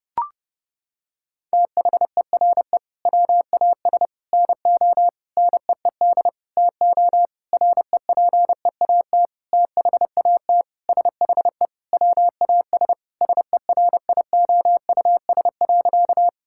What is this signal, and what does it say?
Morse code, heard as a keyed tone, about 32 wpm, 700 Hz. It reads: THERE WAS NO NEED TO REPEAT THAT SHE WAS SERIOUS.
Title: Sentences from Top 1600 Words 30wpm